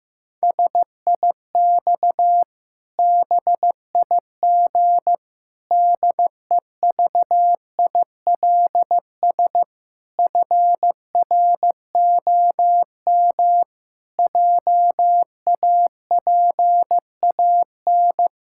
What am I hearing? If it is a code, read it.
Morse code, heard as a keyed tone, about 15 wpm, 700 Hz. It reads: SIX BIG DEVILS FROM JAPAN
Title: Pangram Challenge v2 - 15wpm